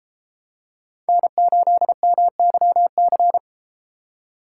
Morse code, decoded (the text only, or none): N8MYC